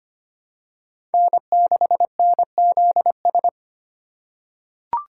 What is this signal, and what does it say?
Morse code, heard as a keyed tone, about 25 wpm, 700 Hz. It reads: N6NZS
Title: USA Call Signs - 4x Character Spacing - 25wpm